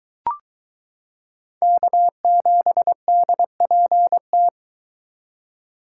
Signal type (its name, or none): Morse code